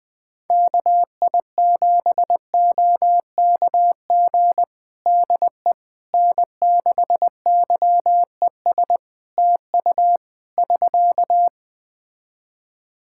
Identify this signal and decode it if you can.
Morse code, heard as a keyed tone, about 20 wpm, 700 Hz. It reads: KI7OKG DE N6YES TU <SK>
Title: Practice QSO - Exchange 4 - Ending - 20wpm